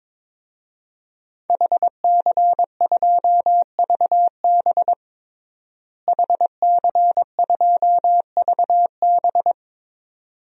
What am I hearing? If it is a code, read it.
Morse code, heard as a keyed tone, about 22 wpm, 700 Hz. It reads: HC2VB HC2VB
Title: DXCC-100 Call Signs - 5 Characters - Repeated 2x - 22wpm